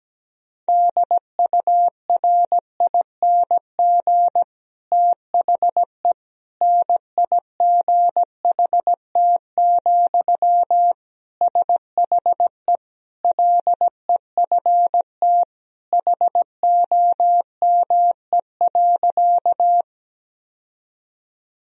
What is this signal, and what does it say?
Morse code, heard as a keyed tone, about 17 wpm, 700 Hz. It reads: DURING THE NIGHT, SHE LEFT HOME.
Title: Sentences from Top 1000 Words - Review Part 2 - 17wpm